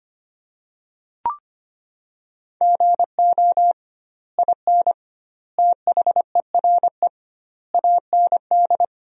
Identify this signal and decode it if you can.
Morse code, heard as a keyed tone, about 25 wpm, 700 Hz. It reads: GO IN THERE AND
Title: 2x Word-Spacing - Sentences from Top 500 Words - 25wpm